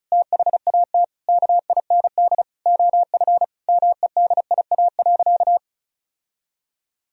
Morse code, heard as a keyed tone, about 36 wpm, 700 Hz. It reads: THAT KIND OF MEDIA.
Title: Sentences from Top 1000 Words - Review Part 4 - 35wpm